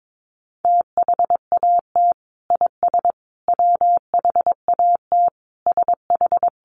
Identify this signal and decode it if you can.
Morse code, heard as a keyed tone, about 22 wpm, 700 Hz. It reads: THAT IS WHAT SH
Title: Sentences from Top 100 Words 22wpm